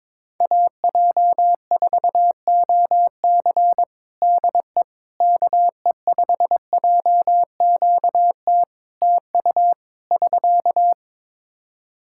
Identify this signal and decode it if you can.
Morse code, heard as a keyed tone, about 22 wpm, 700 Hz. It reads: AJ4OC DE KE5JQT TU <SK>
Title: K1USN SST QSO Practice 22wpm